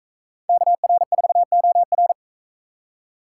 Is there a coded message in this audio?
Yes: Morse code from 0.5 to 2.1 s, heard as a keyed tone, about 43 wpm, 700 Hz. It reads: KR4OR